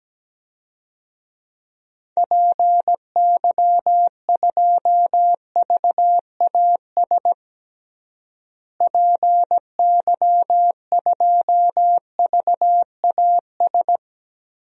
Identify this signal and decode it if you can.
Morse code, heard as a keyed tone, about 17 wpm, 700 Hz. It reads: PY2VAS PY2VAS
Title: DXCC-100 Call Signs - 6 Characters - Repeated 2x - 17wpm